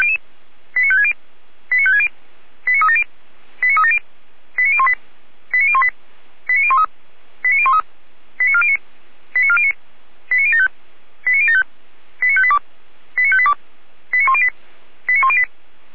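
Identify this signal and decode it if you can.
Five-tone selective call (ZVEI), heard as quick runs of five tones, tones of about 70 ms. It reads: 89560 89560 89370 89370 89018 89018 89013 89013 89409 89409 89075 89075 89582 89582 89108 89108
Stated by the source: ZVEI five-tone selective call received on SDR, fire brigade feuerwehr-orlamuende.org.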